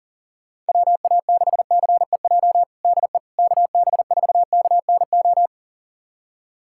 Morse code, heard as a keyed tone, about 40 wpm, 700 Hz. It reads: WA6CEJ DE KB4KNO
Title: USA Callsign DE Callsign - 6 Characters - 40wpm